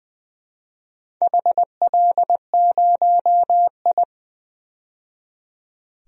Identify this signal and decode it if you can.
Morse code, heard as a keyed tone, about 20 wpm, 700 Hz. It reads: HL0I
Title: DXCC-100 Call Signs - 4 Characters - 20wpm